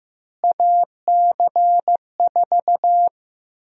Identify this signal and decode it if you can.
Morse code, heard as a keyed tone, about 15 wpm, 700 Hz. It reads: AC4
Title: USA Call Sign Prefixes - Repeated 2x - 15wpm